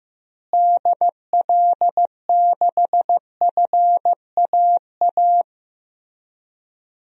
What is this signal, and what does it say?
Morse code, heard as a keyed tone, about 15 wpm, 700 Hz. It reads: DL6FAA
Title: DXCC-100 Call Signs - 6 Characters - Repeated 3x - 15wpm